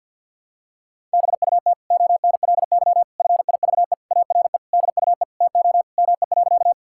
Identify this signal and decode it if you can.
Morse code, heard as a keyed tone, about 51 wpm, 700 Hz. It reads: BUT ONLY FIVE ARE DUE TO ME.
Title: Sentences from Top 1000 Words - Review Part 6 - 50wpm